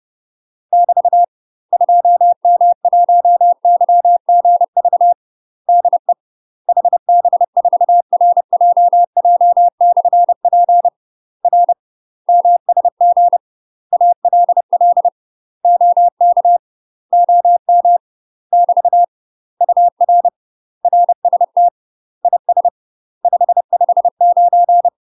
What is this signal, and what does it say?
Morse code, heard as a keyed tone, about 30 wpm, 700 Hz. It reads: <BT> 2M1YGV DE HB4RJJ/P R MSG ALL OK OM <BT> UR RST IS 559
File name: UK MASTER QSOs TRAINING / 30WPM / UK MASTER QSOs 0049